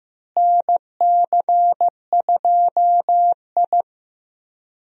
Morse code, heard as a keyed tone, about 15 wpm, 700 Hz. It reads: NC2I
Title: USA Call Signs - 4 Characters - Repeated 2x - 15wpm